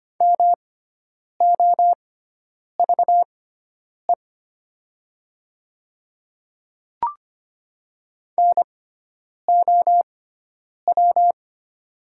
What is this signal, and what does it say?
Morse code, heard as a keyed tone, about 25 wpm, spaced as for 9 wpm, 690 Hz. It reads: MOVE NOW
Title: 6x Character Spacing - Top 200 Words - 25wpm